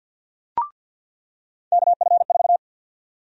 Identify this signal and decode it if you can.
Morse code, heard as a keyed tone, about 53 wpm, 700 Hz. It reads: KF4
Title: USA Call Sign Prefixes - 50wpm